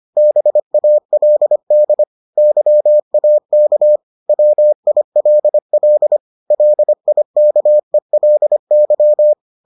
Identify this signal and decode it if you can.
Morse code, heard as a keyed tone, about 25 wpm, 600 Hz. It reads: BALD YAK WILL LIKELY